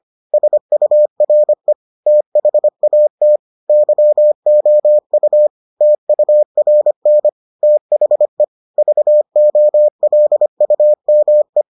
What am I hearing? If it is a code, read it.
Morse code, heard as a keyed tone, about 25 wpm, 600 Hz. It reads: SURE THAT YOU TURN THE VOLUME